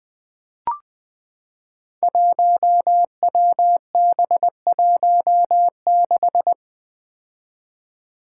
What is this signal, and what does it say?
Morse code, heard as a keyed tone, about 20 wpm, 700 Hz. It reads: 1WB16
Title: Five Challenging Letter-Number Combinations 20wpm